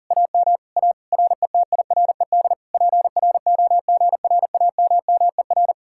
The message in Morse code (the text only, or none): AM A RETIRED PROGRAMMER